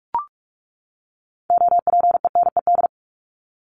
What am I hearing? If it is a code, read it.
Morse code, heard as a keyed tone, about 44 wpm, 700 Hz. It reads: OPENED
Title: ED Suffix - Top 100 Words with Base Words 45wpm